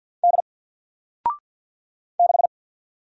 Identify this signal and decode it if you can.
Morse code, heard as a keyed tone, about 48 wpm, 700 Hz. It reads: D 6
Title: Single Character (Letter:Number:Punctuation:Prosign) 50wpm